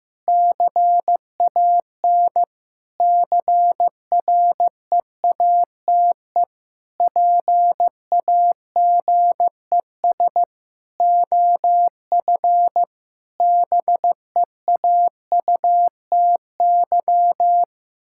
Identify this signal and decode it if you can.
Morse code, heard as a keyed tone, about 15 wpm, 700 Hz. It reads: CAN CREATE PAGES OF BEAUTY
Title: Pangram Challenge v2 - 15wpm